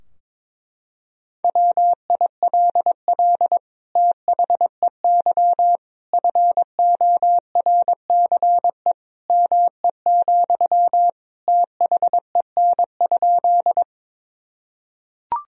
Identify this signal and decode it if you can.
Morse code, heard as a keyed tone, about 22 wpm, 700 Hz. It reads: WILL THEY FORCE ME, THEN?
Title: Sentences from Top 1000 Words - Review Part 3 - 22wpm